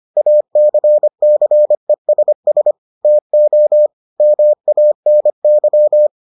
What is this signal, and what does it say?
Morse code, heard as a keyed tone, about 25 wpm, 600 Hz. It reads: ACCESS TO MANY